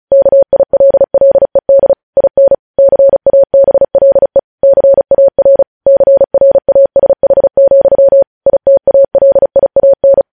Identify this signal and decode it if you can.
Morse code, heard as a keyed tone, about 36 wpm, 550 Hz. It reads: KILLED IN CABLE CAR CRASH, ITALIAN
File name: podcast / News headlines-1x-35x35